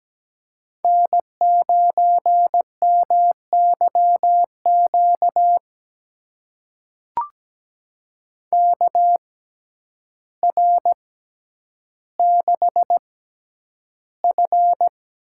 Morse code, heard as a keyed tone, about 17 wpm, spaced as for 9 wpm, 700 Hz. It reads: N9MYQ KR6F
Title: USA Call Signs - 5 Characters - 6x Character Spacing - 17wpm